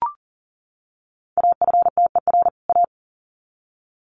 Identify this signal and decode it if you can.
Morse code, heard as a keyed tone, about 38 wpm, 700 Hz. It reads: AFTER A
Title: Sets of 2 Words 40wpm